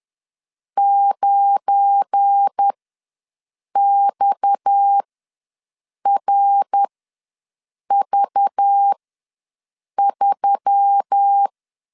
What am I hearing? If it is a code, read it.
Morse code, heard as a keyed tone, about 10 wpm, spaced as for 6 wpm, 800 Hz. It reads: 9XRV3
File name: LEZIONI / 7 GF9Z / 7 GF9 ws 10wpm 4034